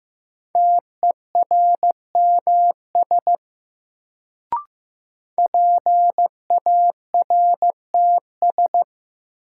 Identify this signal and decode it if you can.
Morse code, heard as a keyed tone, about 15 wpm, 700 Hz. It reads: TERMS PARTS
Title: Sets of 3 Words - New Word Drill - 15wpm